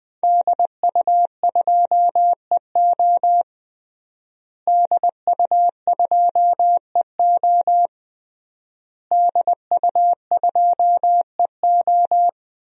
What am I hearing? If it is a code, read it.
Morse code, heard as a keyed tone, about 20 wpm, 700 Hz. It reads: DU2EO DU2EO DU2EO
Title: DXCC-100 Call Signs - 5 Characters - Repeated 3x - 20wpm